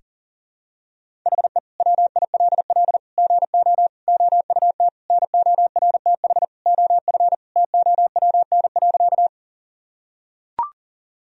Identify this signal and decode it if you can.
Morse code, heard as a keyed tone, about 40 wpm, 700 Hz. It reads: HE WILL GO OUT NORTH OF TOWN.
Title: Sentences from Top 1000 Words - Review Part 1 - 40wpm